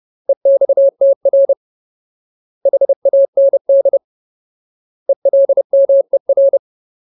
Morse code, heard as a keyed tone, about 30 wpm, 550 Hz. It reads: EXTR HAND ELMER
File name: Ultimate Word List / 30WPM Ultimate